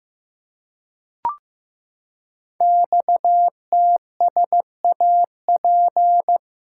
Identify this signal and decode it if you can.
Morse code, heard as a keyed tone, about 15 wpm, 700 Hz. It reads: XTSAP
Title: Code Groups - 5 Letters - 15wpm